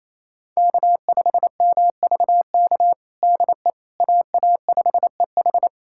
Morse code, heard as a keyed tone, about 27 wpm, 700 Hz. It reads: K5MVK DE AA5EH
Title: USA Callsign DE Callsign - 5 Characters - 28wpm